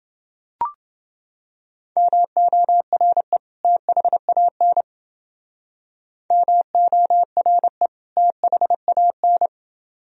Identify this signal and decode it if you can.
Morse code, heard as a keyed tone, about 29 wpm, 700 Hz. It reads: MORE THAN MORE THAN